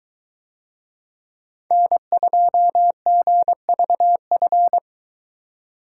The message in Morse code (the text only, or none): N2GVF